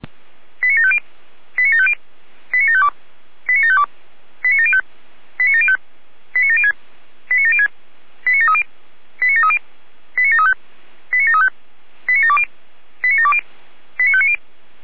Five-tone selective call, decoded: 89750 89750 89752 89752 89705 89705 89706 89706 89730 89730 89735 89735 89720 89720 89590

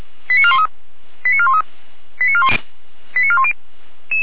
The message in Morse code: none

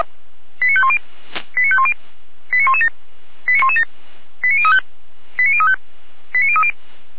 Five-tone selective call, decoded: 89510 89510 89117 89117 89035 89035 89030

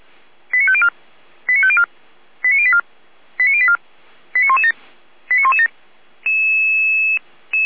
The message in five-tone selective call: 89504 89504 89084 89084 89117 89117

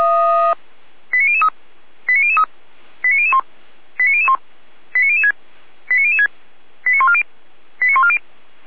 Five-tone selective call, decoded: 89003 89003 89001 89001 89006 89006 89140 89140

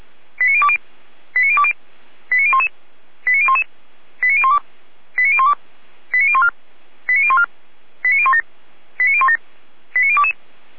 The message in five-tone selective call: 89020 89020 89011 89011 89012 89012 89014 89014 89017 89017 89022

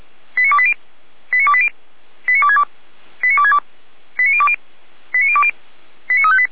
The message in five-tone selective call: 89280 89280 89262 89262 89020 89020 89358

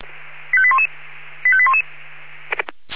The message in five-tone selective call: 85811 85811

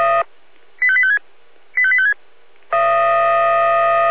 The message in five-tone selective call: 85856 85856

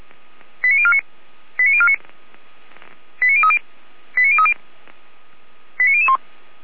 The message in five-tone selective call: 89049 89049 89030 89030 89001

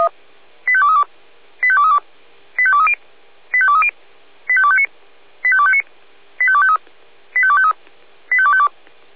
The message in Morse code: none

